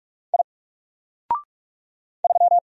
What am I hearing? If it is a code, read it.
Morse code, heard as a keyed tone, about 44 wpm, 700 Hz. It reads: I 3
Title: Single Character (Letter:Number:Punctuation:Prosign) 45wpm